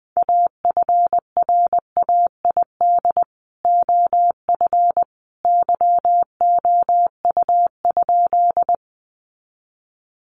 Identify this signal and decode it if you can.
Morse code, heard as a keyed tone, about 20 wpm, 700 Hz. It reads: AFRAID OF YOU?
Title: Sentences from Top 1500 Words 20wpm